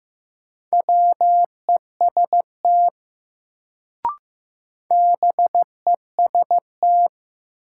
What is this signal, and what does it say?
Morse code, heard as a keyed tone, about 15 wpm, 700 Hz. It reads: WEST BEST
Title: Sentences from Top 400 Words - New Word Drill - 15wpm